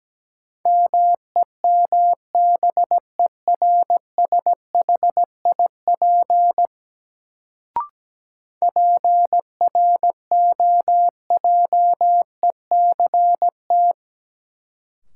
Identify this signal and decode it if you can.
Morse code, heard as a keyed tone, about 17 wpm, 700 Hz. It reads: MEMBERSHIP PROJECT
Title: Sentences from Top 1400 Words - New Word Drill Alt - 17wpm